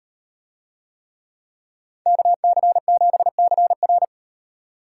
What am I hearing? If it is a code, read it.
Morse code, heard as a keyed tone, about 35 wpm, 700 Hz. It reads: KC7CR